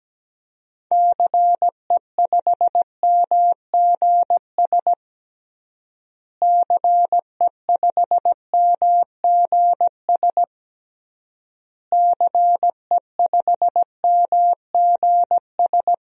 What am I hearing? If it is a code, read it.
Morse code, heard as a keyed tone, about 17 wpm, 700 Hz. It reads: CE5MGS CE5MGS CE5MGS
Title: DXCC-100 Call Signs - 6 Characters - Repeated 3x - 17wpm